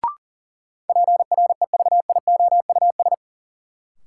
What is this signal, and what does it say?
Morse code, heard as a keyed tone, about 40 wpm, 700 Hz. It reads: PREVIOUS
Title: Sentences from Top 1500 Words - New Word Drill Alt - 40wpm